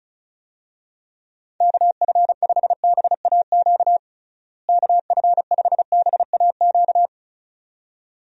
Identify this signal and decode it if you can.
Morse code, heard as a keyed tone, about 34 wpm, 700 Hz. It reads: KF5BAQ KF5BAQ
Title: USA Call Signs - 6 Characters - Repeated 2x - 35wpm